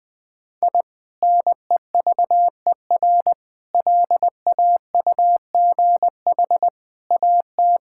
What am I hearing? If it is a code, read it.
Morse code, heard as a keyed tone, about 20 wpm, 700 Hz. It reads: I NEVER LAUGH AT
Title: Sentences from Top 1000 Words - Review Part 6 - 20wpm